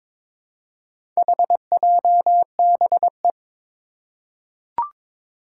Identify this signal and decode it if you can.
Morse code, heard as a keyed tone, about 22 wpm, 700 Hz. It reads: HJBE
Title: Four Challenging Letter-Number Combinations 22wpm